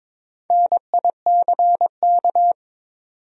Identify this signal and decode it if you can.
Morse code, heard as a keyed tone, about 22 wpm, 700 Hz. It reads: NICK
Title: USA Common Names - Max 4 Letters - 22wpm